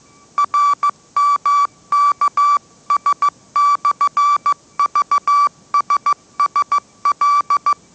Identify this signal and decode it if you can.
Morse code, heard as a keyed tone, about 15 wpm, 1160 Hz. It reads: RMKS/VSSL